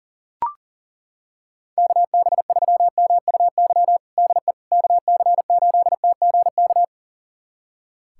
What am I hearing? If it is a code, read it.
Morse code, heard as a keyed tone, about 40 wpm, 700 Hz. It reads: KB3MUY DE KC8TGK
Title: USA Callsign DE Callsign - 6 Characters - 40wpm